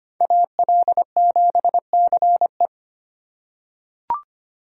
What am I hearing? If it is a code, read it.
Morse code, heard as a keyed tone, about 25 wpm, 700 Hz. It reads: AL7CE
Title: USA Call Signs - 5 Characters - 8x Character Spacing - 25wpm